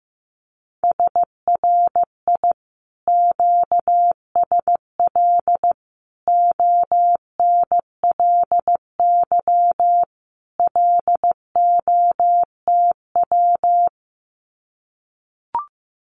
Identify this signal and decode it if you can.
Morse code, heard as a keyed tone, about 15 wpm, 700 Hz. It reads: SRI QSL ONLY LOTW
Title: Common QSO Phrases Encore 15wpm